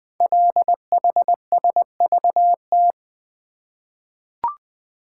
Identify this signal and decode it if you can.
Morse code, heard as a keyed tone, about 20 wpm, 700 Hz. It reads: LHSVT
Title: Code Groups - 5 Letters - 20wpm